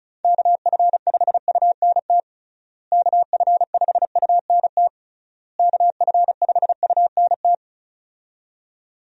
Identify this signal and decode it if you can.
Morse code, heard as a keyed tone, about 36 wpm, 700 Hz. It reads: KF5UNT KF5UNT KF5UNT
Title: USA Call Signs - 6 Characters - Repeated 3x - 35wpm